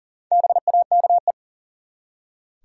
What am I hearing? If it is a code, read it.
Morse code, heard as a keyed tone, about 40 wpm, 700 Hz. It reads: BAKE
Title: Four Letter Words - 40wpm